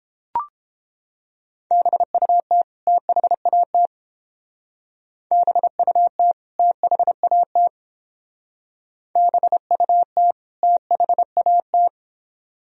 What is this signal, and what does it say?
Morse code, heard as a keyed tone, about 29 wpm, 700 Hz. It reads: BUT THAT BUT THAT BUT THAT